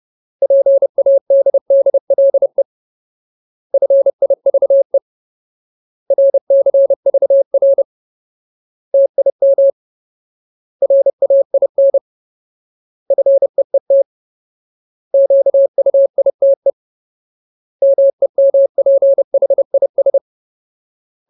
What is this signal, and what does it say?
Morse code, heard as a keyed tone, about 30 wpm, 550 Hz. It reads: PADDLE FIVE RCVR TIM RAIN FEET QUITE MEMPHIS
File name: Ultimate Word List / 30WPM Ultimate